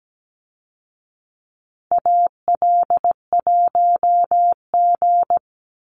Morse code, heard as a keyed tone, about 17 wpm, 700 Hz. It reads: AL1G
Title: Call Signs Repeated 2x - 17wpm